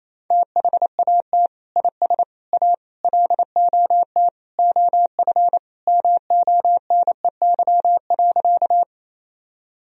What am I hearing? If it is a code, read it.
Morse code, heard as a keyed tone, about 28 wpm, 700 Hz. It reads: THAT IS A LOT OF MONEY.
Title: Sentences from Top 1000 Words - Review Part 3 - 28wpm